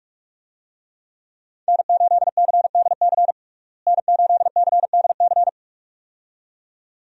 Morse code, heard as a keyed tone, about 41 wpm, 700 Hz. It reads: N8CDC N8CDC
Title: USA Call Signs - 5 Characters - Repeated 2x - 45wpm